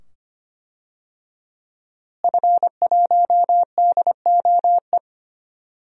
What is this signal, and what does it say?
Morse code, heard as a keyed tone, about 25 wpm, 700 Hz. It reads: F1DOE
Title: DXCC-100 Call Signs - 5 Characters - Repeated 2x - 25wpm